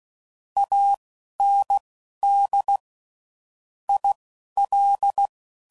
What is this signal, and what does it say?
Morse code, heard as a keyed tone, about 16 wpm, spaced as for 11 wpm, 800 Hz. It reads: AND IL
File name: CW Code Groups @10WPM / QSOs / QSO kc7haa